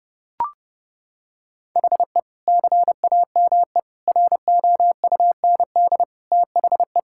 Morse code, heard as a keyed tone, about 30 wpm, 700 Hz. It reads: HE CAME ROUND THE
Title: Sentences from Top 1600 Words 30wpm